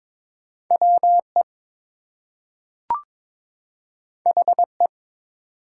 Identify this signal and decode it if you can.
Morse code, heard as a keyed tone, about 22 wpm, 700 Hz. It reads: WE HE
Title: Two Letter Words - 22wpm